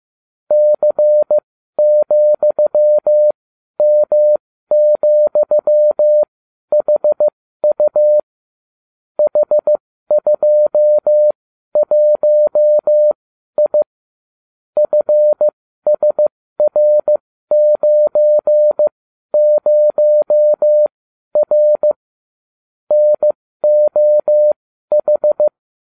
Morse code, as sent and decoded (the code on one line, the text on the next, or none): -.-. --..-- -- --..-- .... ..- / .... ..--- .---- .. / ..-. ... .-. ----. ----- .-. / -. --- ....
C,M,HU H21I FSR90R NOH